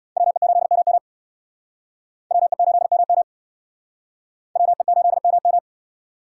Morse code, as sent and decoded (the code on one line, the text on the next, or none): .-- . ---.. -- -- / .-- . ---.. -- -- / .-- . ---.. -- --
WE8MM WE8MM WE8MM